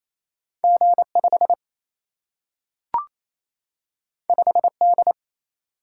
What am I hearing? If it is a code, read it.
Morse code, heard as a keyed tone, about 28 wpm, 700 Hz. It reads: G5 5D
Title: Single-Letter Single-Number 28wpm